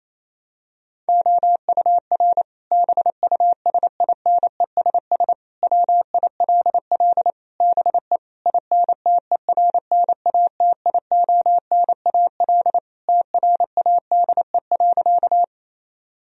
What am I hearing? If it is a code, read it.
Morse code, heard as a keyed tone, about 28 wpm, 700 Hz. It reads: OUR BUSINESS WILL BE INTERNATIONAL TRADE.
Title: Sentences from Top 1000 Words - Review Part 4 - 28wpm